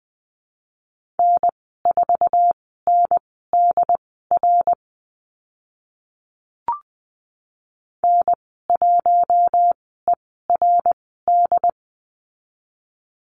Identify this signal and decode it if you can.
Morse code, heard as a keyed tone, about 20 wpm, spaced as for 14 wpm, 700 Hz. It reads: N4NDR N1ERD
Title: 069 Lesson 12g - Letter d - Call Signs 20wpm